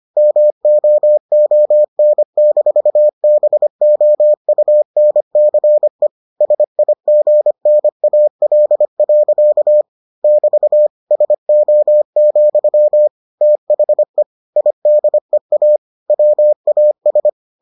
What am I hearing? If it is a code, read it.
Morse code, heard as a keyed tone, about 25 wpm, 600 Hz. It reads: MOON-BOUNCE SIGNAL. <BT> SO, THE IDEA WAS